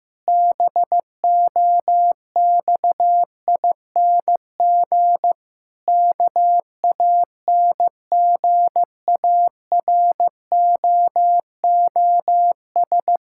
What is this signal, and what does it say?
Morse code, heard as a keyed tone, about 15 wpm, 700 Hz. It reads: BOXING KANGAROOS
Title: Pangram Challenge v2 - 15wpm